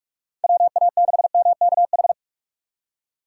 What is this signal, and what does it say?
Morse code, heard as a keyed tone, about 41 wpm, 700 Hz. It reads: WA6MKH
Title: USA Call Signs - 6 Characters - 2x Character Spacing - 45wpm